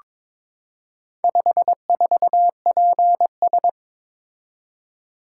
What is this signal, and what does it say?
Morse code, heard as a keyed tone, about 22 wpm, 700 Hz. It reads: 54PS
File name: practice / 22wpm / Four Letter-Number Combinations - 22wpm